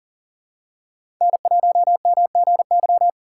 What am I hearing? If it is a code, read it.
Morse code, heard as a keyed tone, about 40 wpm, 700 Hz. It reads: N1MGY
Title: USA Call Signs - 5 Characters - 4x Character Spacing - 40wpm